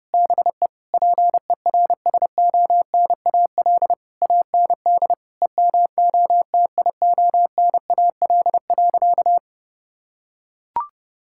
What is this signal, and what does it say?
Morse code, heard as a keyed tone, about 30 wpm, 700 Hz. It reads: BE PERSONAL AND EMOTIONAL.
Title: Sentences from Top 1000 Words - Review Part 5 - 30wpm